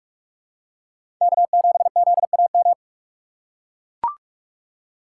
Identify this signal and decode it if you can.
Morse code, heard as a keyed tone, about 44 wpm, 700 Hz. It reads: K7ZAM
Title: USA Call Signs - 5 Characters - 6x Character Spacing - 45wpm